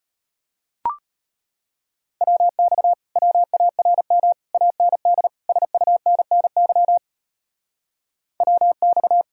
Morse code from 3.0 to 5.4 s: WARM AND

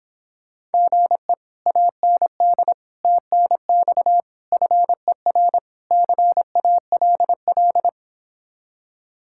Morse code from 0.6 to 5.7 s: GE AND TNX FER

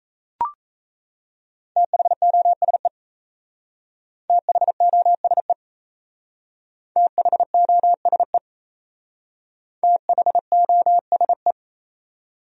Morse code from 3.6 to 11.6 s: THOSE THOSE THOSE

